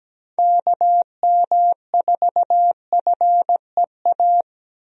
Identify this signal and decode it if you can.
Morse code, heard as a keyed tone, about 17 wpm, 700 Hz. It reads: KM4FEA
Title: USA Call Signs - 6 Characters - Repeated 3x - 17wpm